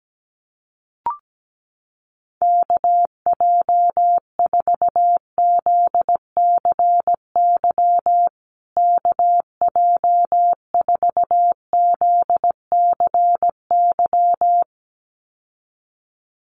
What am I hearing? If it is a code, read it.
Morse code, heard as a keyed tone, about 17 wpm, 700 Hz. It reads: KJ4ZCY KJ4ZCY
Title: USA Call Signs Repeated 2x 17wpm